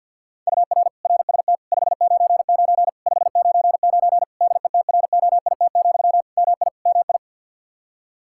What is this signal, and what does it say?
Morse code, heard as a keyed tone, about 50 wpm, 700 Hz. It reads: UR RST 599 599 DETROIT, MI MI
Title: Practice QSO - Exchange 1 - Beginning - 50wpm